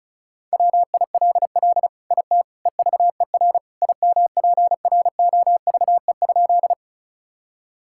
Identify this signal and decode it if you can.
Morse code, heard as a keyed tone, about 36 wpm, 690 Hz. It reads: WILL IT EVER IMPROVE?
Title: Sentences from Top 1000 Words - Review Part 5 - 35wpm